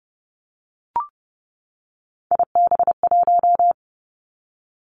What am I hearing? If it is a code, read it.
Morse code, heard as a keyed tone, about 30 wpm, 700 Hz. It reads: IB1
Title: Three Challenging Letter-Number Combinations 30wpm